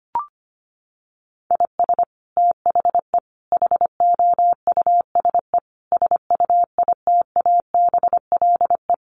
Morse code, heard as a keyed tone, about 25 wpm, 700 Hz. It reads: IS THE HOUSE SUITABLE
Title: Sentences from Top 1400 Words 25wpm